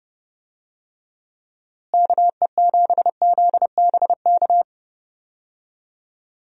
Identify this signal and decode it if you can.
Morse code, heard as a keyed tone, about 30 wpm, 700 Hz. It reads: KE7ZBK
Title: USA Call Signs - 6 Characters - Repeated 3x - 30wpm